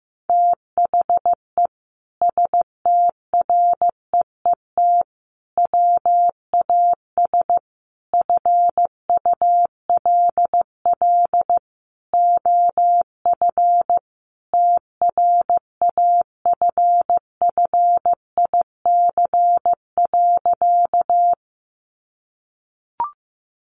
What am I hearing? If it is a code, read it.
Morse code, heard as a keyed tone, about 15 wpm, 700 Hz. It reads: THE STREET WAS FULL OF TRAFFIC.
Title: Sentences from Top 1300 Words 15wpm